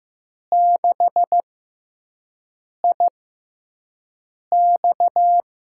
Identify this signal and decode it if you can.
Morse code, heard as a keyed tone, about 15 wpm, spaced as for 5 wpm, 700 Hz. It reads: 6IX
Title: USA Call Signs - 6 Characters - 6x Character Spacing - 15wpm